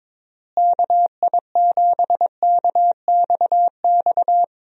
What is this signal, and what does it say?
Morse code, heard as a keyed tone, about 22 wpm, 700 Hz. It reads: KI7KXX
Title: USA Call Signs - 6 Characters - Repeated 3x - 22wpm